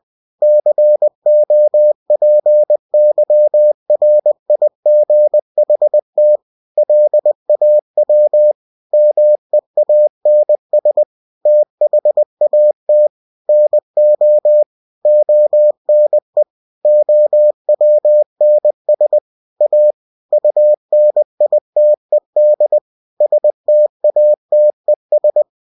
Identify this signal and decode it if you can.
Morse code, heard as a keyed tone, about 20 wpm, 600 Hz. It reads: COPYRIGHT LAW MEANS THAT NO ONE OWNS A UNITED STATES
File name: cw-books / winnie-the-pooh / winnie-20wpm-page-0112